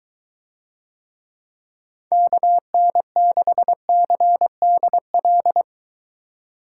Morse code, heard as a keyed tone, about 23 wpm, 700 Hz. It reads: KN6CDL